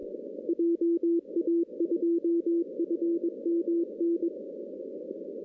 Morse code, heard as a keyed tone, about 21 wpm, 340 Hz, with radio noise. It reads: JA2FMN
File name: 2016 arrl dx cw / 2016-ARRL-DX-CW-K5TR- - 0-25